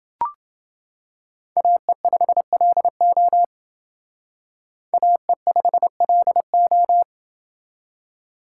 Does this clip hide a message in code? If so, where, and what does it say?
Yes: Morse code from 1.6 to 7.0 s, heard as a keyed tone, about 29 wpm, 700 Hz. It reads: AE5LO AE5LO